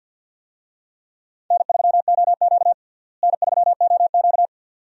Morse code, heard as a keyed tone, about 53 wpm, 700 Hz. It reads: N3OQ N3OQ
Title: USA Call Signs - 4 Characters - Repeated 2x - 50wpm